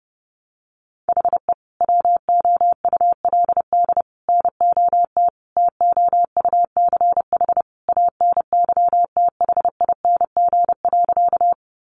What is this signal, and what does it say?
Morse code, heard as a keyed tone, about 30 wpm, 700 Hz. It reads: HE WOULD NOT TOUCH ANYTHING.
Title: Sentences from Top 1200 Words 30wpm